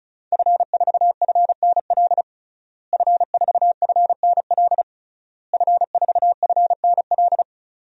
Morse code, heard as a keyed tone, about 36 wpm, 700 Hz. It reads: F4FNL F4FNL F4FNL
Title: DXCC-100 Call Signs - 5 Characters - Repeated 3x - 35wpm